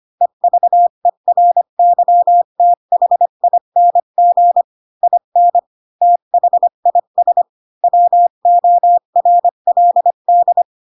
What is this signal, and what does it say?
Morse code, heard as a keyed tone, about 25 wpm, 700 Hz. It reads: EVERYTHING IN THIS WORLD
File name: Pinocchio / Pinocchio ch7 20 25 ik1hge